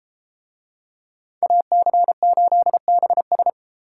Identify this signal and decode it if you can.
Morse code, heard as a keyed tone, about 32 wpm, 700 Hz. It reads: AC8BS